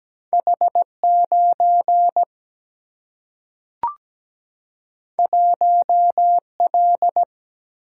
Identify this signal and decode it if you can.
Morse code, heard as a keyed tone, about 17 wpm, 700 Hz. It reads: H9 1L
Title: Single-Letter Single-Number 17wpm